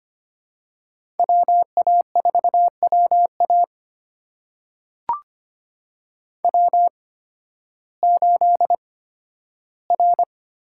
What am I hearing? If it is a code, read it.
Morse code, heard as a keyed tone, about 25 wpm, spaced as for 13 wpm, 700 Hz. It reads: WA4WA W8R
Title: USA Call Signs - 5 Characters - 8x Character Spacing - 25wpm